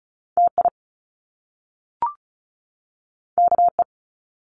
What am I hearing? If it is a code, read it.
Morse code, heard as a keyed tone, about 36 wpm, 700 Hz. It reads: TI KE
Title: Top Two Letter Combinations in Words 35wpm